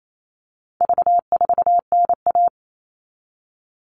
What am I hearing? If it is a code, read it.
Morse code, heard as a keyed tone, about 28 wpm, 700 Hz. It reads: V4NA
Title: All Cut Numbers - 4-Digit 28wpm